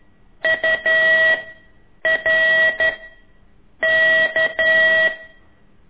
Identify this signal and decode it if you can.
Morse code, heard as a keyed tone, about 11 wpm, 1950 Hz. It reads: U R K
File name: ik1weg2 / CORSI CW